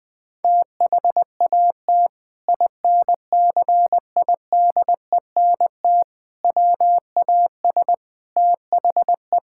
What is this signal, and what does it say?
Morse code, heard as a keyed tone, about 20 wpm, 700 Hz. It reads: THAT INCIDENT WAS THE
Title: Sentences with 8-Word Phrases - 20wpm